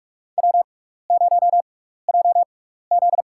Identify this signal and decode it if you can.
Morse code, heard as a keyed tone, about 46 wpm, spaced as for 15 wpm, 700 Hz. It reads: W0JZ
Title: USA Call Signs - 4 Characters - 6x Character Spacing - 45wpm